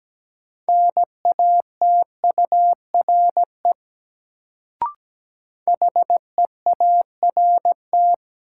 Morse code, heard as a keyed tone, about 17 wpm, 700 Hz. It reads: NATURE HEART
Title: Sets of 3 Words - New Word Drill - 17wpm